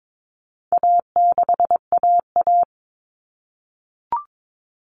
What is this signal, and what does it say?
Morse code, heard as a keyed tone, about 22 wpm, 700 Hz. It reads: A6AA
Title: All Cut Numbers - 4-Digit 22wpm